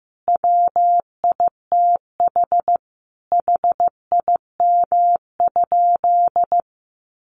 Morse code, heard as a keyed tone, about 15 wpm, 700 Hz. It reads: WITH HIM?
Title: Sentences from Top 600 Words 15wpm